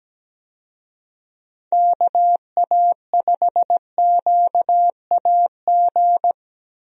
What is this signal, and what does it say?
Morse code, heard as a keyed tone, about 17 wpm, 700 Hz. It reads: KA5QAG
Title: USA Call Signs - 6 Characters - Repeated 3x - 17wpm